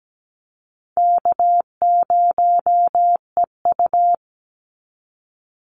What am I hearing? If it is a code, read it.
Morse code, heard as a keyed tone, about 17 wpm, 700 Hz. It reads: K0EU
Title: Call Signs Repeated 3x - 17wpm